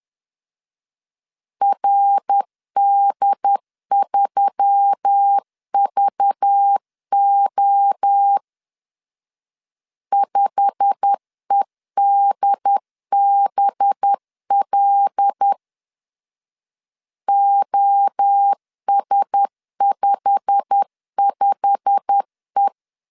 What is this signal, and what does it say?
Morse code, heard as a keyed tone, about 10 wpm, 800 Hz. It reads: RD3VO 5EDBL OS55E
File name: LEZIONI / 8 YJ45 / 8 YJ45 10wpm 2549